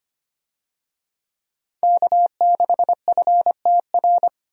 Morse code, heard as a keyed tone, about 25 wpm, 700 Hz. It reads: K6FTR
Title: USA Call Signs - 5 Characters - Repeated 3x - 25wpm